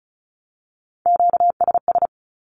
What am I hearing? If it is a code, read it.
Morse code, heard as a keyed tone, about 36 wpm, 700 Hz. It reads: QSS
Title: Q-Codes with Spoken Meaning - 35wpm